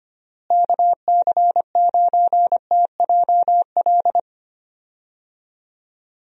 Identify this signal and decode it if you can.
Morse code, heard as a keyed tone, about 25 wpm, 700 Hz. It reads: KC9TJL
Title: USA Call Signs - 6 Characters - Repeated 3x - 25wpm